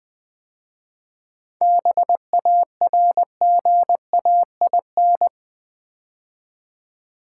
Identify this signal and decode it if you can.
Morse code, heard as a keyed tone, about 20 wpm, 690 Hz. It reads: BARGAIN
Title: 7 Letter Words - First 3 Letters Given - 20wpm